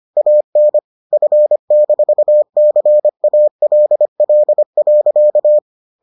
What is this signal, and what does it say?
Morse code, heard as a keyed tone, about 25 wpm, 600 Hz. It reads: AN F-CALL.